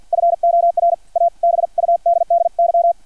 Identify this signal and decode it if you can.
Morse code, heard as a keyed tone, about 50 wpm, spaced as for 44 wpm, 680 Hz. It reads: FOR A BUDDY
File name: archived / CRSnet